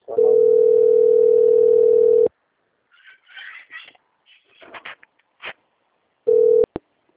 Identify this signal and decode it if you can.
Ringback tone (North America), 440 + 480 Hz.
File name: out- - - - - .91774